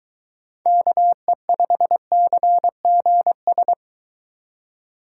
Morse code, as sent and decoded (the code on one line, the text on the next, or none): -.- . ..... -.-. --. ...
KE5CGS